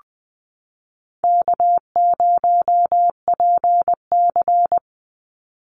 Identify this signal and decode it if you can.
Morse code, heard as a keyed tone, about 20 wpm, 700 Hz. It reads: K0PC
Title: Easy Call Signs Repeated 3x - 20wpm